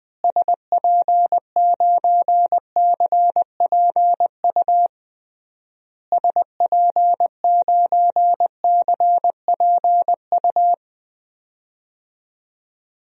Morse code, heard as a keyed tone, about 20 wpm, 700 Hz. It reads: SP9CPU SP9CPU
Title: DXCC-100 Call Signs - 6 Characters - Repeated 3x - 20wpm